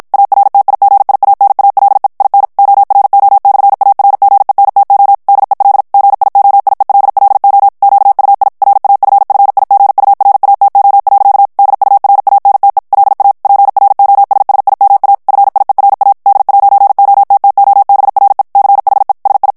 Morse code, heard as a keyed tone, about 56 wpm, 800 Hz. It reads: ARTIGIANALE IN ORO/ARGENTO DEL GIOIELLO QUI RAFFIGURATO. DURANTE LA PROSSIMA FIERA D'OTTOBRE CHE SI
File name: morse mp3 / ITALIEN texte clair 1 vitesse 50 mots minute